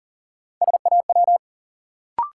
Morse code, heard as a keyed tone, about 40 wpm, 700 Hz. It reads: SAW
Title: Sentences from Top 300 Words - New Word Drill - 40wpm